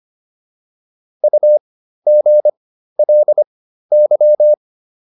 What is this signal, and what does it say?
Morse code, heard as a keyed tone, about 25 wpm, spaced as for 13 wpm, 600 Hz. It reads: UGLY